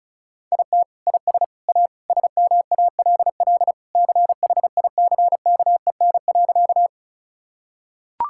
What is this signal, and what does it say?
Morse code, heard as a keyed tone, about 34 wpm, 700 Hz. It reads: IT IS A SMALL CHICKEN.
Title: Sentences from Top 1000 Words - Review Part 5 - 35wpm